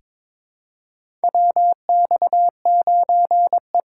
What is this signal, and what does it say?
Morse code, heard as a keyed tone, about 22 wpm, 700 Hz. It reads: WX9E
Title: USA Call Signs - 4x Character Spacing - 22wpm